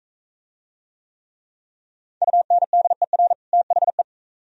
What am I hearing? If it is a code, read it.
Morse code, heard as a keyed tone, about 43 wpm, 700 Hz. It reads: UNDER THE